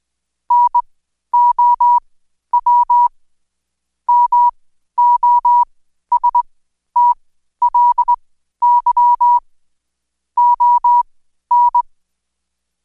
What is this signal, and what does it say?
Morse code, heard as a keyed tone, about 20 wpm, spaced as for 12 wpm, 970 Hz. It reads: NOW MOSTLY ON
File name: Code Quick / AudioDownld2 / Seg121-Day26-DayDrill 26